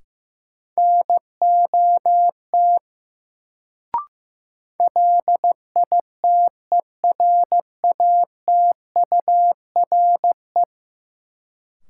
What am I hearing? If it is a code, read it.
Morse code, heard as a keyed tone, about 15 wpm, 700 Hz. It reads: NOT LITERATURE
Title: Sentences from Top 600 Words - New Word Drill - 15wpm